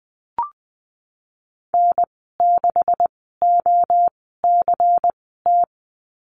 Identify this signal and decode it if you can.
Morse code, heard as a keyed tone, about 20 wpm, spaced as for 15 wpm, 700 Hz. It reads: N6OCT
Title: 147 Lesson 24g - Letter y - Call Signs 20wpm